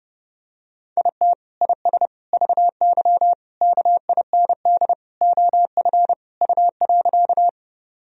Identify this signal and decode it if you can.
Morse code, heard as a keyed tone, about 30 wpm, 700 Hz. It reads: IT IS VY KIND OF U.
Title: Sentences from Top 200 Words and Abbreviations 30wpm